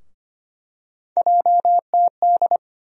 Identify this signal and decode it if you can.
Morse code, heard as a keyed tone, about 25 wpm, 700 Hz. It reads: JTD
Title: USA Call Sign Postfixes - 3 Letters 25wpm